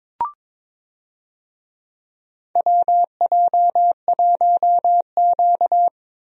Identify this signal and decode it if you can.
Morse code, heard as a keyed tone, about 22 wpm, 700 Hz. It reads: WJ1Q
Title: USA Call Signs - 4 Characters - 22wpm